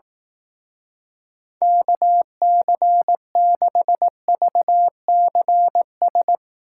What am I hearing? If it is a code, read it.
Morse code, heard as a keyed tone, about 18 wpm, 700 Hz. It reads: KC6VCS
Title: Speed Racing - USA Call Signs 15wpm